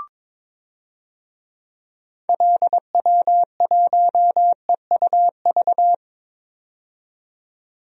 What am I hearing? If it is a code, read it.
Morse code, heard as a keyed tone, about 22 wpm, 700 Hz. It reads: LW1EUV
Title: DXCC-100 Call Signs - 6 Characters - 22wpm